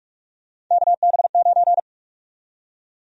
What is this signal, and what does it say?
Morse code, heard as a keyed tone, about 46 wpm, 700 Hz. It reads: KB9
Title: USA Call Sign Prefixes - 45wpm